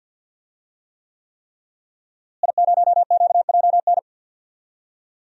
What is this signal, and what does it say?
Morse code, heard as a keyed tone, about 53 wpm, 700 Hz. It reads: I0QJN
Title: DXCC-100 Call Signs - 5 Characters - Repeated 3x - 50wpm